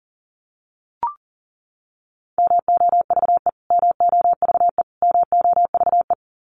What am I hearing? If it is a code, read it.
Morse code, heard as a keyed tone, about 40 wpm, 700 Hz. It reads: MOVE MOVE MOVE
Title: Top 200 Words Repeated 40wpm